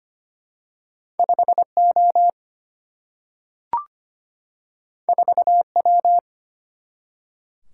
Morse code, heard as a keyed tone, about 25 wpm, 700 Hz. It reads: 5O 4W
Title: Single-Letter Single-Number 25wpm